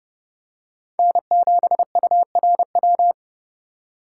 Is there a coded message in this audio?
Yes: Morse code from 1.0 to 3.1 s, heard as a keyed tone, about 28 wpm, 700 Hz. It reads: N7URW